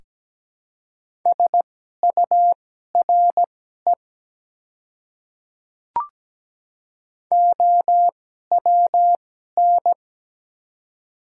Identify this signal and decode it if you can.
Morse code, heard as a keyed tone, about 17 wpm, spaced as for 12 wpm, 700 Hz. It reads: SURE OWN
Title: 2x Character Spacing - Top 300 Words - 17wpm